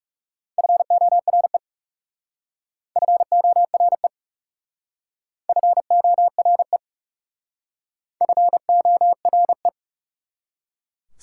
Morse code, heard as a keyed tone, about 36 wpm, 700 Hz. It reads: FORE FORE FORE FORE
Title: Speed-Racing Prefixes and Suffixes 30wpm